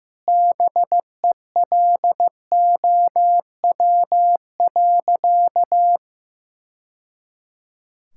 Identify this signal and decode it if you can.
Morse code, heard as a keyed tone, about 16 wpm, 700 Hz. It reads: BELOW.
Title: Sentences from Top 1000 Words - Review Part 5 - 15wpm